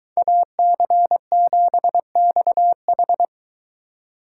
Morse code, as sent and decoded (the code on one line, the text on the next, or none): .- -.-. --... -..- ....
AC7XH